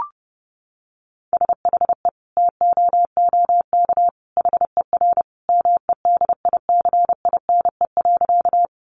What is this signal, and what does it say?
Morse code, heard as a keyed tone, about 30 wpm, 700 Hz. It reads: SHE TOOK HER MEDICINE.
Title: Sentences from Top 900 Words 30wpm